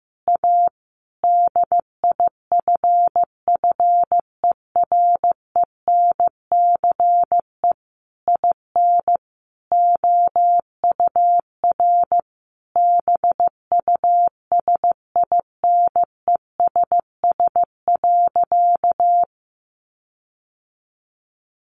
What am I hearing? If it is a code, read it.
Morse code, heard as a keyed tone, about 15 wpm, 700 Hz. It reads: A DIFFERENCE IN OUR BUSINESS.
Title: Sentences from Top 1400 Words 15wpm